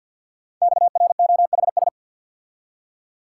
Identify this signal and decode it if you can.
Morse code, heard as a keyed tone, about 50 wpm, 700 Hz. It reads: XROHS
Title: Code Groups - 5 Letters - 50wpm